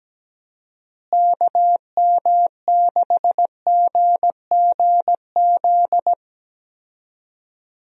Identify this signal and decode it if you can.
Morse code, heard as a keyed tone, about 17 wpm, 700 Hz. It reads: KM6GGZ
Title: USA Call Signs - 6 Characters - Repeated 2x - 17wpm